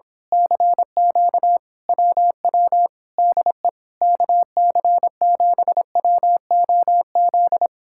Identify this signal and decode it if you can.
Morse code, heard as a keyed tone, about 26 wpm, 700 Hz. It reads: CQ WW DE KC7WOZ
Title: Speed-Racing Common QSO Phrases Encore - 22wpm